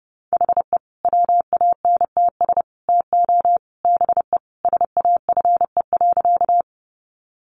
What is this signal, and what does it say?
Morse code, heard as a keyed tone, about 30 wpm, 700 Hz. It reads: HE WANTS TO BE SAFE.
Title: Additional Easy Phrases 30wpm